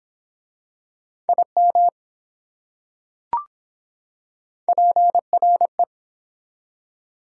Morse code, heard as a keyed tone, about 23 wpm, 700 Hz. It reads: IM PRE